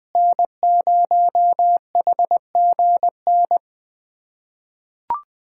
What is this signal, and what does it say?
Morse code, heard as a keyed tone, about 20 wpm, 700 Hz. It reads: N0HGN
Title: USA Callsign DE Callsign - 5 Characters - 20wpm